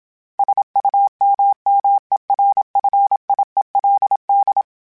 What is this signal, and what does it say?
Morse code, heard as a keyed tone, about 27 wpm, 800 Hz. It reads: SUMMERFIELD
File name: fromSMs / FromSMs mp3s / Cities@25